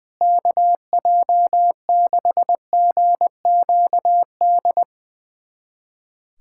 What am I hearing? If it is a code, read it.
Morse code, heard as a keyed tone, about 20 wpm, 700 Hz. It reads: KJ6GQD
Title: USA Callsign DE Callsign 20wpm